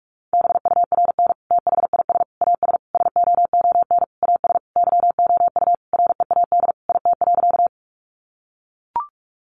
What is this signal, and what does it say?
Morse code, heard as a keyed tone, about 44 wpm, 700 Hz. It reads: BURN THIS AS SOON AS YOU READ IT.
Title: Sentences from Top 1300 Words 45wpm